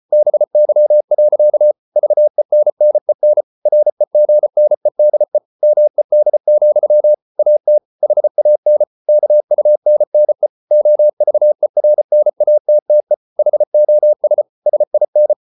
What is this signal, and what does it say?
Morse code, heard as a keyed tone, about 34 wpm, 600 Hz. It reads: BY. VENNEN REGNEDE MED, AT HAN KUNNE OVERNATTE HOS SIN
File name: Ferie QRQ 34wpm